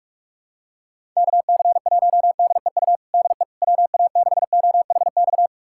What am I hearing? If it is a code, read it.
Morse code, heard as a keyed tone, about 46 wpm, 700 Hz. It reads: KC1DEU DE WA6OHX
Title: USA Callsign DE Callsign - 6 Characters - 45wpm